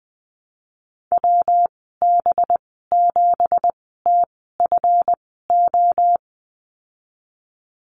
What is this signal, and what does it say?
Morse code, heard as a keyed tone, about 20 wpm, spaced as for 15 wpm, 700 Hz. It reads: WB7TFO
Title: 193 Lesson 31g - Letter v - Call Signs 20wpm